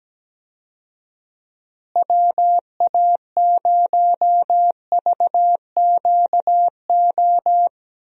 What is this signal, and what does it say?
Morse code, heard as a keyed tone, about 17 wpm, 700 Hz. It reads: WA0VQO
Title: USA Call Signs - 6 Characters - Repeated 3x - 17wpm